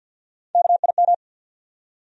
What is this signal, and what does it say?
Morse code, heard as a keyed tone, about 48 wpm, 700 Hz. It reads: KIM
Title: USA Common Names - Max 4 Letters - 50wpm